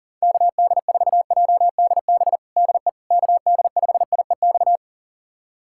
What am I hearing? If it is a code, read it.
Morse code, heard as a keyed tone, about 40 wpm, 700 Hz. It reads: KD4JDB DE KD5IEX
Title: USA Callsign DE Callsign 40wpm